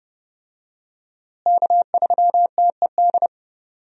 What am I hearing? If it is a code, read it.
Morse code, heard as a keyed tone, about 30 wpm, 700 Hz. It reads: K3TED
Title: USA Call Signs - 5 Characters - Repeated 3x - 30wpm